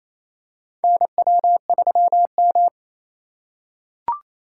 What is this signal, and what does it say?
Morse code, heard as a keyed tone, about 28 wpm, 700 Hz. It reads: NW3M
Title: USA Call Signs - 4 Characters - 2x Character Spacing - 28wpm